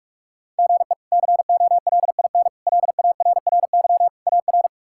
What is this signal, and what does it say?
Morse code, heard as a keyed tone, about 46 wpm, 700 Hz. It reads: GE COLIN LARRY AR
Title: K1USN SST QSO Practice - 45wpm